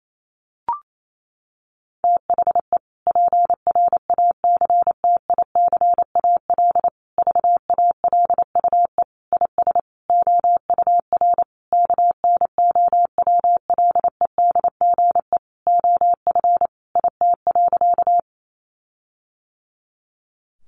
Morse code, heard as a keyed tone, about 28 wpm, 700 Hz. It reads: THE PRACTICAL VALUE IS OUR KNOWLEDGE OF IT.
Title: Sentences from Top 1400 Words 28wpm